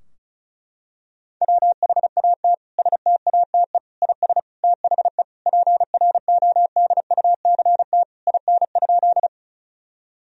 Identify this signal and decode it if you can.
Morse code, heard as a keyed tone, about 36 wpm, 700 Hz. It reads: WHAT STATE IS THE PRODUCT IN?
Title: Sentences from Top 1000 Words - Review Part 2 - 35wpm